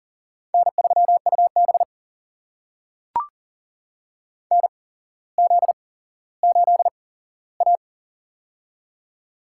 Morse code, heard as a keyed tone, about 39 wpm, spaced as for 17 wpm, 700 Hz. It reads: N3UB NZ8A
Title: USA Call Signs - 4 Characters - 8x Character Spacing - 40wpm